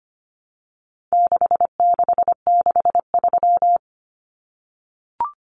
Morse code, heard as a keyed tone, about 25 wpm, 700 Hz. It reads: 6663
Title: Numbers - 4-Digit 25wpm